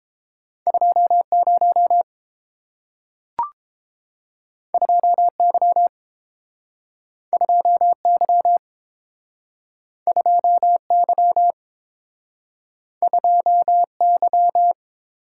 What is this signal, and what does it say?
Morse code, heard as a keyed tone, about 31 wpm, 700 Hz. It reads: 20 2Y 2Y 2Y 2Y